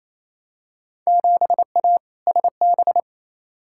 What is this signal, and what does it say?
Morse code, heard as a keyed tone, about 28 wpm, 700 Hz. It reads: 7A SB
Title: Field Day Exchanges Encore - 28wpm